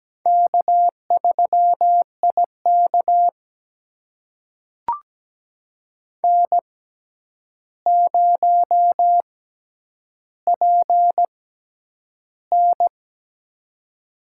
Morse code, heard as a keyed tone, about 17 wpm, spaced as for 9 wpm, 700 Hz. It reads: K3IK N0PN
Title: USA Call Signs - 4 Characters - 6x Character Spacing - 17wpm